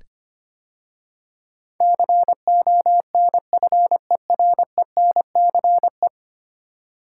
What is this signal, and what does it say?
Morse code, heard as a keyed tone, about 25 wpm, 700 Hz. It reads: CONFERENCE
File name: 10 Letter Words - First 3 Letters Given - 25wpm